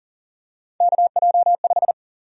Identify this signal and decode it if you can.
Morse code, heard as a keyed tone, about 40 wpm, 700 Hz. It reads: KJ5
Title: USA Call Sign Prefixes - 40wpm